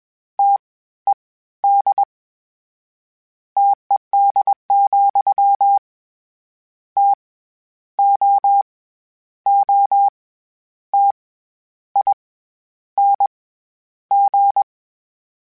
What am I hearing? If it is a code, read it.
Morse code, heard as a keyed tone, about 21 wpm, 800 Hz. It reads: T E D TED, T O O T I N G
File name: Variable Spacing / Variable Spacing-mp3s / T AOEs @20